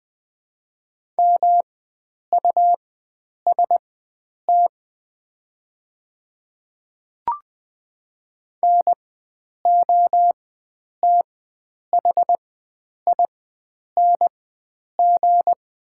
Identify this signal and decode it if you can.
Morse code, heard as a keyed tone, about 20 wpm, spaced as for 9 wpm, 700 Hz. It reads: MUST NOTHING
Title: 4x Character Spacing - Top 500 Words - 20wpm